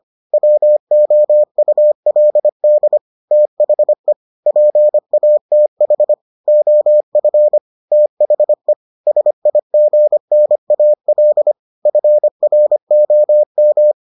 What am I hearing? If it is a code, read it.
Morse code, heard as a keyed tone, about 25 wpm, 600 Hz. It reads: WOULD THE PATH OF THE SIGNAL FROM